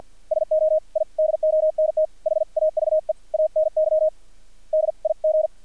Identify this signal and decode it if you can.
Morse code, heard as a keyed tone, about 50 wpm, spaced as for 44 wpm, 630 Hz. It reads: SO I DONT HAVE ANY DIG
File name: archived / CRSnet